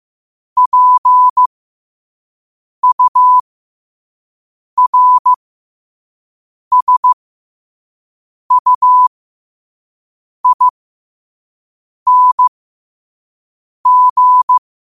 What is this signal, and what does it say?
Morse code, heard as a keyed tone, about 15 wpm, spaced as for 5 wpm, 1000 Hz. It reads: PURSUING